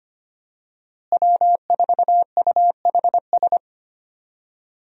Morse code, heard as a keyed tone, about 25 wpm, 700 Hz. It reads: W4UHS
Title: USA Call Signs - 5 Characters - Repeated 2x - 25wpm